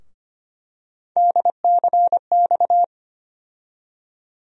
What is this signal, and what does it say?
Morse code, heard as a keyed tone, about 25 wpm, 700 Hz. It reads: DCX
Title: USA Call Sign Postfixes - 3 Letters 25wpm